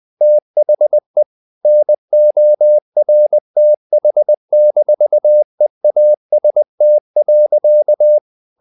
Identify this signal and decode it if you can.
Morse code, heard as a keyed tone, about 20 wpm, 600 Hz. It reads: THE NORTH-EAST.
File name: the old man and the sea / the old man and the sea part 1 20wpm